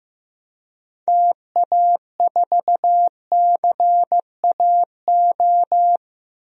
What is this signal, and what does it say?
Morse code, heard as a keyed tone, about 15 wpm, 700 Hz. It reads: TA4CAO
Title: DXCC-100 Call Signs - 6 Characters - Repeated 2x - 15wpm